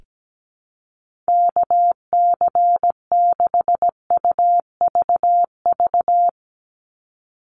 Morse code, heard as a keyed tone, about 17 wpm, 700 Hz. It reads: KC6UVV
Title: USA Call Signs 17wpm